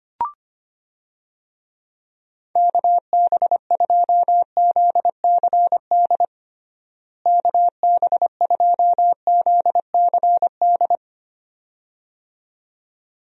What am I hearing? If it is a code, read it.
Morse code, heard as a keyed tone, about 25 wpm, 700 Hz. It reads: KB2ZCD KB2ZCD
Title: USA Call Signs - 6 Characters - Repeated 2x - 25wpm